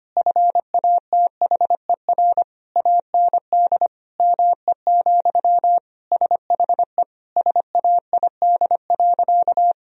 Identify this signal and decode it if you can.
Morse code, heard as a keyed tone, about 25 wpm, 700 Hz. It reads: FATHER AND ME, SHE SAID.
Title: Sentences from Top 1000 Words - Review Part 3 - 25wpm